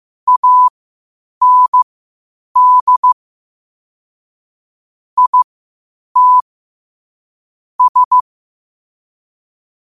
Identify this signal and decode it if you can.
Morse code, heard as a keyed tone, about 15 wpm, spaced as for 8 wpm, 1000 Hz. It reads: AND IT S
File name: qotdfiles / 8wpm